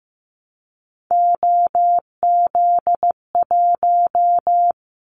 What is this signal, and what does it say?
Morse code, heard as a keyed tone, about 15 wpm, 700 Hz. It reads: OZ1
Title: Callsign Prefixes for Top 100 DXCC - 15wpm